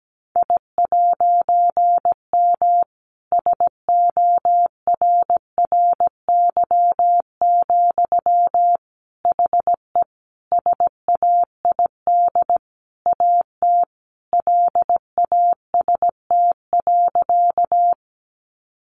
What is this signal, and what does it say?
Morse code, heard as a keyed tone, about 17 wpm, 700 Hz. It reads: I'M SORRY, HE SAID AT LAST.
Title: A Sound Of Thunder 17wpm